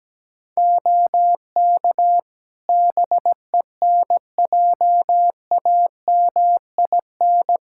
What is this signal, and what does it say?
Morse code, heard as a keyed tone, about 17 wpm, 700 Hz. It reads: OK BENJAMIN
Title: K1USN SST QSO Practice 17wpm